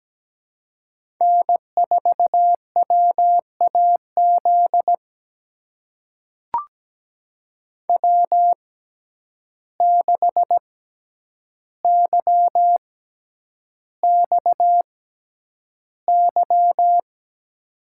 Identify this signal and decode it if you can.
Morse code, heard as a keyed tone, about 17 wpm, spaced as for 9 wpm, 700 Hz. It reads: N4WAZ W6YXY
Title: USA Call Signs - 6x Character Spacing - 17wpm